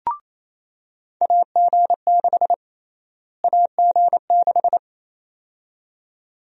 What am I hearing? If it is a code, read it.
Morse code, heard as a keyed tone, about 27 wpm, 700 Hz. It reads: AG6 AG6
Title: USA Call Sign Prefixes - Repeated 2x - 28wpm